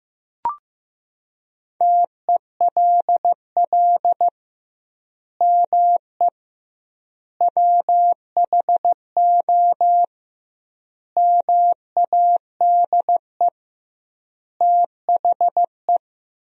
Morse code, heard as a keyed tone, about 15 wpm, 700 Hz. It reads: TELL ME WHO MADE THE
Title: 2x Word-Spacing - Sentences from Top 200 Words - 15wpm